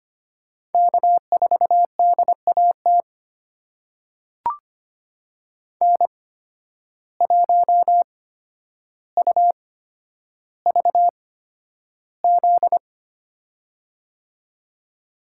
Morse code, heard as a keyed tone, about 25 wpm, spaced as for 11 wpm, 700 Hz. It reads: K4DAT N1UVZ
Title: USA Call Signs - 5 Characters - 8x Character Spacing - 25wpm